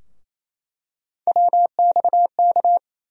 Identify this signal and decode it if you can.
Morse code, heard as a keyed tone, about 28 wpm, 700 Hz. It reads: WXK
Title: USA Call Sign Postfixes - 3 Letters 28wpm